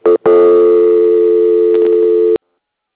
Dial tone (North America), 350 + 440 Hz.